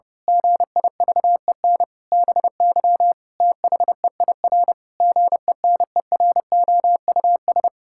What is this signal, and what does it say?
Morse code, heard as a keyed tone, about 30 wpm, 700 Hz. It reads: GIVEN BY THEIR GENEROUS
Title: Pangram Challenge v2 - 30wpm